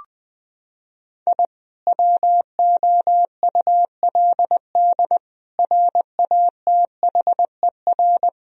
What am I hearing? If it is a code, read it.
Morse code, heard as a keyed tone, about 20 wpm, 700 Hz. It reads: I WOULD RATHER
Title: Sentences from Top 1000 Words - Review Part 4 - 20wpm